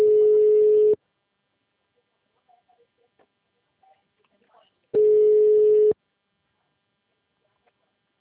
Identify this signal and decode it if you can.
Ringback tone (Europe, 425 Hz), with speech.
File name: out- -2016- - - .54163